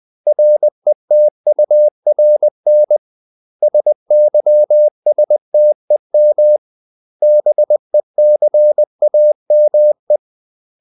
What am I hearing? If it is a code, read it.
Morse code, heard as a keyed tone, about 20 wpm, 600 Hz. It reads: RETURN SYSTEM BECAME
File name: CW / Practice / 500MostCommonEnglishWords20